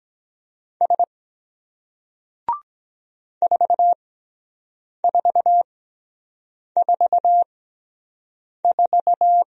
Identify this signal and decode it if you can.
Morse code, heard as a keyed tone, about 23 wpm, spaced as for 7 wpm, 700 Hz. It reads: S 4444